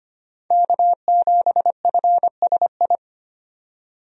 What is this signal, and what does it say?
Morse code, heard as a keyed tone, about 25 wpm, 700 Hz. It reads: K7FSI
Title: USA Call Signs - 5 Characters - 25wpm